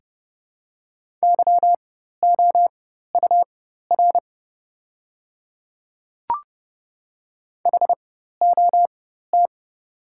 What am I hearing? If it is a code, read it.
Morse code, heard as a keyed tone, about 30 wpm, spaced as for 14 wpm, 700 Hz. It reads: YOUR HOT
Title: 4x Character Spacing - Top 100 Words - 30wpm